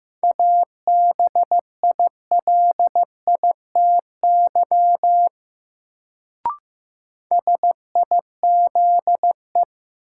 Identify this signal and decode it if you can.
Morse code, heard as a keyed tone, about 15 wpm, 700 Hz. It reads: ABILITY SIZE
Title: Sets of 3 Words - New Word Drill - 15wpm